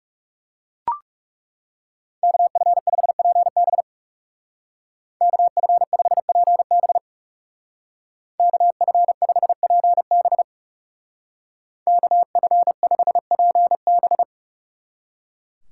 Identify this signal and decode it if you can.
Morse code, heard as a keyed tone, about 35 wpm, 700 Hz. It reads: KF5PB KF5PB KF5PB KF5PB
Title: Speed Racing - USA 2x2 Call Signs 30wpm